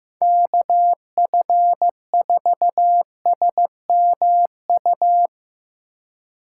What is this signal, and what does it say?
Morse code, heard as a keyed tone, about 17 wpm, 700 Hz. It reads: KF4SMU
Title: Speed Racing - USA 2x3 Call Signs 15wpm